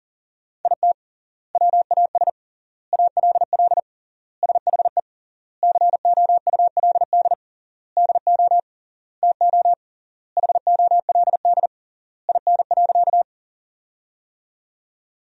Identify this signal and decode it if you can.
Morse code, heard as a keyed tone, about 40 wpm, 700 Hz. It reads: IT WAS ALL SHE COULD DO TO HOLD IN.
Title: 3x Word-Spacing - Sentences from Top 400 Words - 40wpm